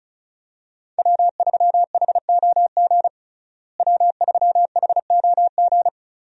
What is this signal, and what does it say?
Morse code, heard as a keyed tone, about 34 wpm, 700 Hz. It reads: W3HOG W3HOG
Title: USA Call Signs - 5 Characters - Repeated 3x - 35wpm